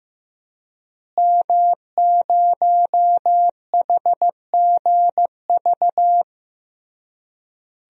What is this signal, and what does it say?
Morse code, heard as a keyed tone, about 16 wpm, 700 Hz. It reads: M0HGV
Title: DXCC-100 Call Signs - 5 Characters - Repeated 3x - 15wpm